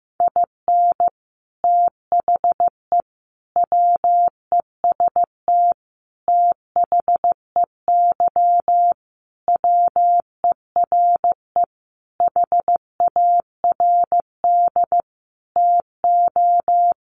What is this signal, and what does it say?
Morse code, heard as a keyed tone, about 15 wpm, 700 Hz. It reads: IN THE WEST THEY WERE HARD TO
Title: Sentences from Top 1300 Words 15wpm